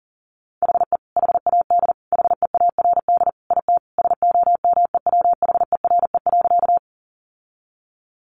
Morse code, heard as a keyed tone, about 40 wpm, 700 Hz. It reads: HE HAD HEARD IT SOMEWHERE.
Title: Sentences from Top 1400 Words 40wpm